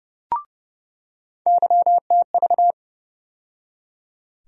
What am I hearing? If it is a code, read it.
Morse code, heard as a keyed tone, about 30 wpm, 700 Hz. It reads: YTV
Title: USA Call Sign Postfixes - 3 Letters 30wpm